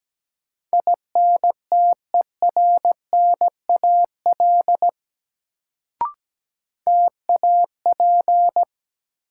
Morse code, heard as a keyed tone, about 17 wpm, 700 Hz. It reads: INTERNAL TAP
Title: Sentences from Top 1400 Words - New Word Drill Alt - 17wpm